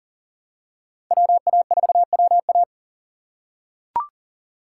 Morse code, heard as a keyed tone, about 40 wpm, 700 Hz. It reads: WA4WA
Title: USA Call Signs - 4x Character Spacing - 40wpm